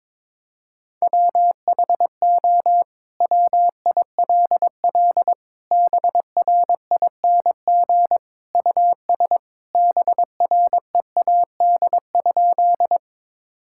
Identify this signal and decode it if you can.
Morse code, heard as a keyed tone, about 22 wpm, 700 Hz. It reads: WHO WILL BRING US BREAD?
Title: Sentences from Top 1600 Words 22wpm